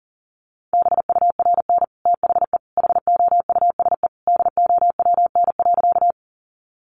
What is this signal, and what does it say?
Morse code, heard as a keyed tone, about 37 wpm, 700 Hz. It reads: BURN THE HOUSE DOWN.
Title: Sentences from Top 1300 Words 40wpm